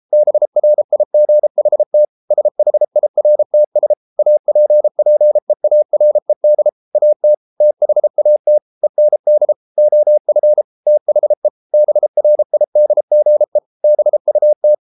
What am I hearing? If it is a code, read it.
Morse code, heard as a keyed tone, about 33 wpm, 600 Hz. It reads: BRIGHT SHIRTS APPEARED AT THAT END OF THE BRIDGE BUT